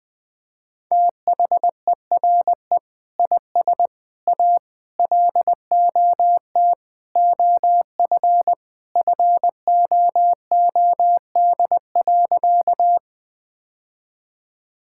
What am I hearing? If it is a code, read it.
Morse code, heard as a keyed tone, about 21 wpm, 700 Hz. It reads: THERE IS A LOT OF FOOD.
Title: Sentences with 3-Word Phrases - Spoken Beginning - 20wpm